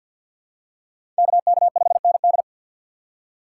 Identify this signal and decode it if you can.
Morse code, heard as a keyed tone, about 46 wpm, 700 Hz. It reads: KK5ND
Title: USA Call Signs - 5 Characters - 4x Character Spacing - 50wpm